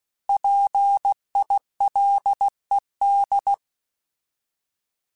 Morse code, heard as a keyed tone, about 16 wpm, 800 Hz. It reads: PILED
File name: Alphabetical / Alphabetical mp3s / Ps @15